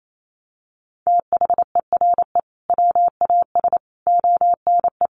Morse code, heard as a keyed tone, about 28 wpm, 700 Hz. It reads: THERE WAS ONE
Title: Sentences from Top 1500 Words 28wpm